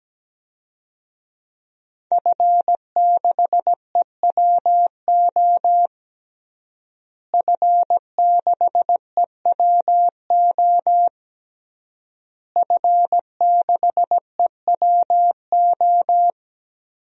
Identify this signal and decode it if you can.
Morse code, heard as a keyed tone, about 17 wpm, 700 Hz. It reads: F6EWO F6EWO F6EWO
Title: DXCC-100 Call Signs - 5 Characters - Repeated 3x - 17wpm